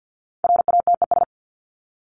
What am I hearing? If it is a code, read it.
Morse code, heard as a keyed tone, about 48 wpm, 700 Hz. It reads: RATES
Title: ES:S Suffix - Top 100 Words with Base Words 50wpm